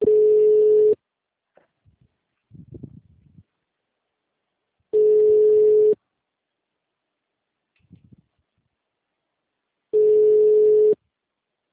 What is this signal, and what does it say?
Ringback tone (Europe, 425 Hz).
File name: out- -2011- - - .40773